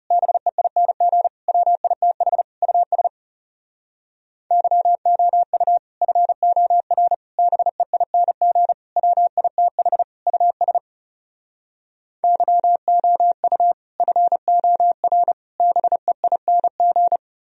Morse code, heard as a keyed tone, about 40 wpm, 700 Hz. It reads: BEING WITH US YOU FOR BEING WITH US YOU FOR BEING